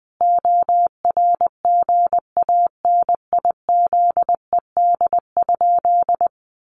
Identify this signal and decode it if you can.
Morse code, heard as a keyed tone, about 19 wpm, 700 Hz. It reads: ORGANIZED?
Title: Sentences from Top 1300 Words 20wpm